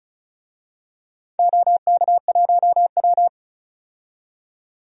Morse code, heard as a keyed tone, about 36 wpm, 700 Hz. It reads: OK1W
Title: DXCC-100 Call Signs - 4 Characters - 35wpm